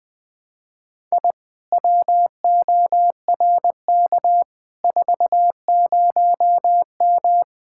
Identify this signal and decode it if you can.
Morse code, heard as a keyed tone, about 21 wpm, 700 Hz. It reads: I WORK 40M
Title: Speed-Racing Common QSO Phrases Encore - 20wpm